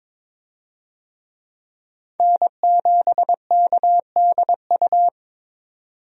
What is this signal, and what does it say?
Morse code, heard as a keyed tone, about 22 wpm, 700 Hz. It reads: N7KDU
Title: USA Call Signs - 5 Characters - 22wpm